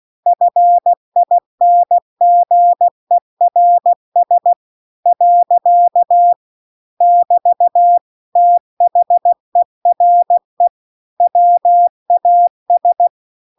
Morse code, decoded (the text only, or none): FINGERS . <BT> THERE WAS